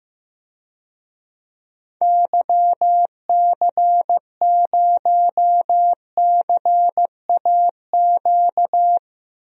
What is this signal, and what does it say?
Morse code, heard as a keyed tone, about 15 wpm, 700 Hz. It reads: YC0CAQ
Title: DXCC-100 Call Signs - 6 Characters - Repeated 2x - 15wpm